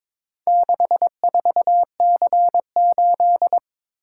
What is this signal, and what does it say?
Morse code, heard as a keyed tone, about 22 wpm, 700 Hz. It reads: 64C8
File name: practice / 22wpm / Four Letter-Number Combinations - 22wpm